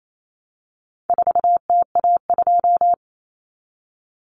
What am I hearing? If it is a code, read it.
Morse code, heard as a keyed tone, about 28 wpm, 700 Hz. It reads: 4TA2
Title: Common Cut Numbers - 4-Digit 28wpm